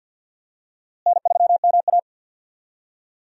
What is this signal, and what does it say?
Morse code, heard as a keyed tone, about 46 wpm, 700 Hz. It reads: N3MA
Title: USA Call Signs - 4 Characters - Repeated 3x - 50wpm